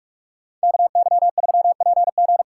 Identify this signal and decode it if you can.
Morse code, heard as a keyed tone, about 58 wpm, 700 Hz. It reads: KY3PG